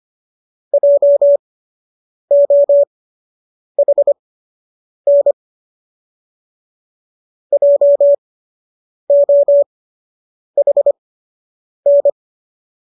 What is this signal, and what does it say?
Morse code, heard as a keyed tone, about 25 wpm, spaced as for 8 wpm, 570 Hz. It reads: JOHN JOHN
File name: qso006 08